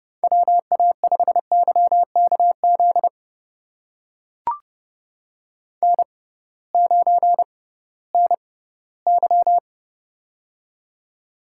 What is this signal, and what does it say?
Morse code, heard as a keyed tone, about 28 wpm, spaced as for 17 wpm, 700 Hz. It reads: WA5YKZ N9NY
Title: USA Call Signs - 6x Character Spacing - 30wpm